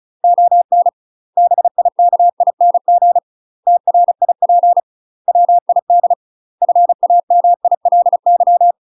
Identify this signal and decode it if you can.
Morse code, heard as a keyed tone, about 35 wpm, 700 Hz. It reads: ON BIKING TRIP WID FAMILY
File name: UK MASTER QSOs TRAINING / 35WPM / UK MASTER QSOs 0033